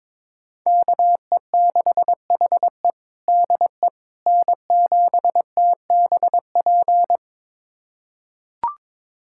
Morse code, heard as a keyed tone, about 22 wpm, 700 Hz. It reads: KE6HE DE N7TBP
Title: USA Callsign DE Callsign - 5 Characters - 22wpm